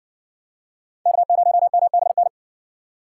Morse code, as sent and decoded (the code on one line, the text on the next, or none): -.- ----- -- -... -.
K0MBN